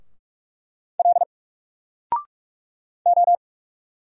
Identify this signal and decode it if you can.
Morse code, heard as a keyed tone, about 45 wpm, 700 Hz. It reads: L O
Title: Single Letter-Number 45wpm